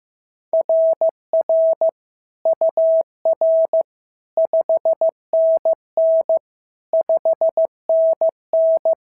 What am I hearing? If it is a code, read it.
Morse code, heard as a keyed tone, about 14 wpm, 650 Hz. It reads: RR UR 5NN 5NN
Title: SOTA QSO Practice 15wpm